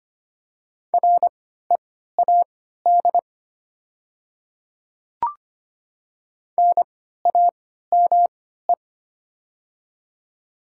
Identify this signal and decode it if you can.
Morse code, heard as a keyed tone, about 25 wpm, spaced as for 14 wpm, 700 Hz. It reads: READ NAME
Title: 3x Character Spacing - Top 200 Words - 25wpm